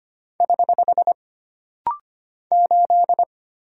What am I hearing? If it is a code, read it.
Morse code, heard as a keyed tone, about 25 wpm, 700 Hz. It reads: <HH> 8
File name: Single Character (Letter:Number:Punctuation:Prosign) - Rapid-Fire - 25wpm